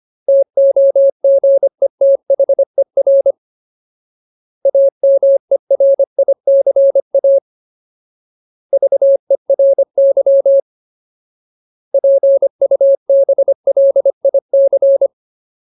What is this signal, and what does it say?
Morse code, heard as a keyed tone, about 25 wpm, 550 Hz. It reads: TOGETHER AMERICA VERY PUBLIC
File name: Ultimate Word List / 25WPM Ultimate